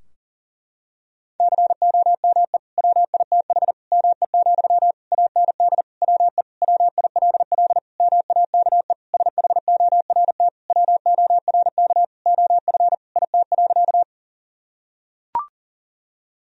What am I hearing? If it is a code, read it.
Morse code, heard as a keyed tone, about 40 wpm, 700 Hz. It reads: COME WITH ME, AND WE WILL MAKE SHORT WORK OF IT.
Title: Sentences from Top 1000 Words - Review Part 2 - 40wpm